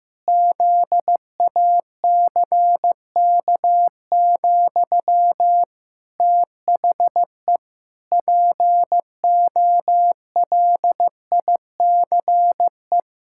Morse code, heard as a keyed tone, about 14 wpm, 700 Hz. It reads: ZACK, THE POLICE
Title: Pangram Challenge v2 - 15wpm